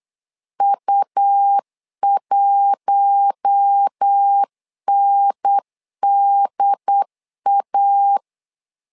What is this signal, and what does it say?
Morse code, heard as a keyed tone, about 8 wpm, 800 Hz. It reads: U1NDA
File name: LEZIONI / 5 LBVC / 5 L 8wpm 2831